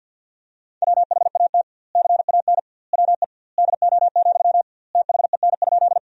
Morse code, heard as a keyed tone, about 53 wpm, 700 Hz. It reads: WHAT CAN WE DO, THEN?
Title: 2x Word-Spacing - Sentences from Top 100 Words - 50wpm